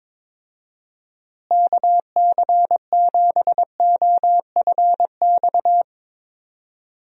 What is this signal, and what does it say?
Morse code, heard as a keyed tone, about 22 wpm, 700 Hz. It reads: KC7OFX
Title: USA Call Signs - 6 Characters - Repeated 3x - 22wpm